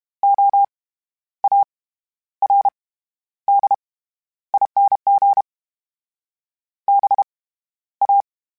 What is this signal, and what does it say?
Morse code, heard as a keyed tone, about 32 wpm, 800 Hz. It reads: O A R D ING B A
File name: Special INGs / Special INGs-mp3s / SlowToFaster ing @30